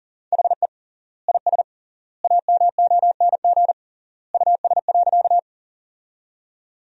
Morse code, heard as a keyed tone, about 40 wpm, 700 Hz. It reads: HE IS AMONG US.
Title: 3x Word-Spacing - Sentences from Top 500 Words - 40wpm